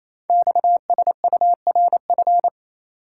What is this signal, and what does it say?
Morse code, heard as a keyed tone, about 27 wpm, 700 Hz. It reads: XSURF
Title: Code Groups - 5 Letters - 28wpm